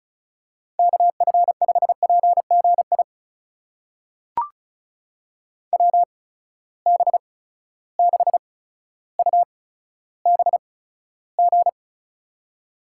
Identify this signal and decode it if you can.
Morse code, heard as a keyed tone, about 36 wpm, spaced as for 15 wpm, 700 Hz. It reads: KF5PGI WB6UBG
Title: USA Call Signs - 6 Characters - 8x Character Spacing - 35wpm